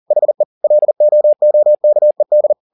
Morse code, heard as a keyed tone, about 40 wpm, 600 Hz. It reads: HE LOOKED